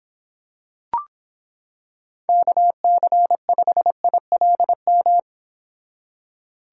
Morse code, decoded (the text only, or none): KC5ILM